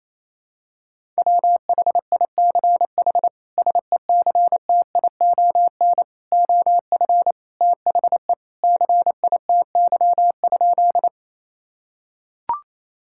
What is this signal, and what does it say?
Morse code, heard as a keyed tone, about 27 wpm, 700 Hz. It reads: WHICH SECTION OF THE CITY?
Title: Sentences from Top 1000 Words - Review Part 5 - 28wpm